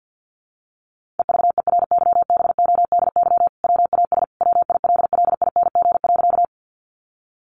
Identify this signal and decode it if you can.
Morse code, heard as a keyed tone, about 50 wpm, 700 Hz. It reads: EVERYBODY WAS WILLING.
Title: Sentences from Top 1400 Words 50wpm